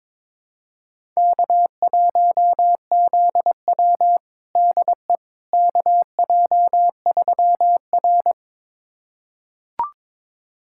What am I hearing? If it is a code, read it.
Morse code, heard as a keyed tone, about 22 wpm, 700 Hz. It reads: K1ZW DE KJ3R
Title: USA Callsign DE Callsign - 4 Characters - 22wpm